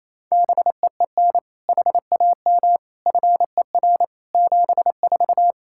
Morse code, heard as a keyed tone, about 28 wpm, 700 Hz. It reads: BEEN HAM FER 74
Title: Practice QSO - Exchange 3 - More Info - 28wpm